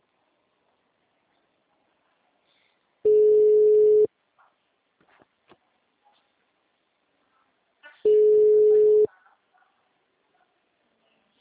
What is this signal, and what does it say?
Ringback tone (Europe, 425 Hz), with speech.